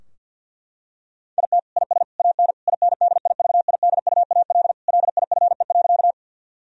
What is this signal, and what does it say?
Morse code, heard as a keyed tone, about 50 wpm, 700 Hz. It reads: IT IS AN INDIVIDUAL LIFE.
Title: Sentences from Top 1000 Words - Review Part 4 - 50wpm